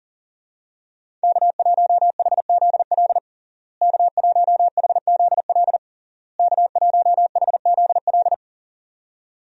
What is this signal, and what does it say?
Morse code, heard as a keyed tone, about 37 wpm, 700 Hz. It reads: K1HZL K1HZL K1HZL
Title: USA Call Signs - 5 Characters - Repeated 3x - 40wpm